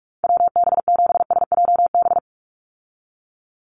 Morse code, heard as a keyed tone, about 44 wpm, 700 Hz. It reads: WB7SJB
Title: USA Call Signs 45wpm